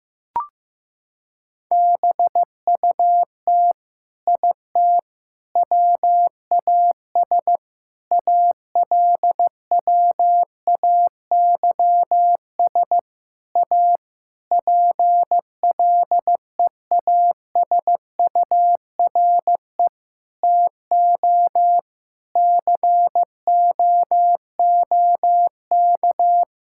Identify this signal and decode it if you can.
Morse code, heard as a keyed tone, about 15 wpm, 700 Hz. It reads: BUT IT WAS ALWAYS A PLEASURE TO COOK
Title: Sentences from Top 1600 Words 15wpm